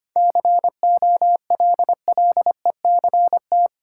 Morse code, heard as a keyed tone, about 25 wpm, 700 Hz. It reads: COLLECT
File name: Easy Sentences with Hints - 25wpm